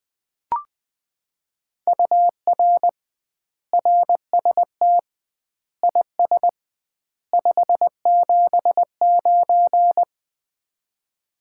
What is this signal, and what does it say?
Morse code, heard as a keyed tone, about 20 wpm, 700 Hz. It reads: UR RST IS 579
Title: Common QSO Phrases - 2x Word-Spacing - 20wpm